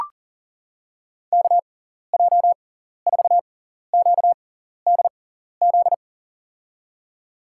Morse code, heard as a keyed tone, about 40 wpm, spaced as for 14 wpm, 700 Hz. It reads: KJ4QDZ
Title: USA Call Signs - 6x Character Spacing - 40wpm